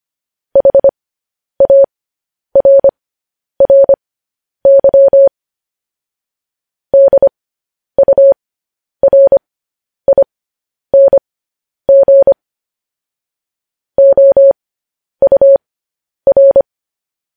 Morse code, heard as a keyed tone, about 25 wpm, spaced as for 10 wpm, 550 Hz. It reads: HARRY DURING OUR
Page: News Headlines in Morse Code at 10 WPM